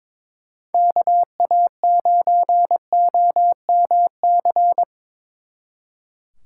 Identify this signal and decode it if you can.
Morse code, heard as a keyed tone, about 25 wpm, 700 Hz. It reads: KA9OMC
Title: Speed Racing - USA Call Signs 22wpm